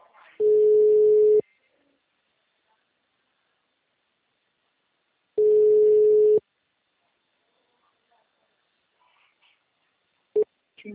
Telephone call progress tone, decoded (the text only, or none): ringback tone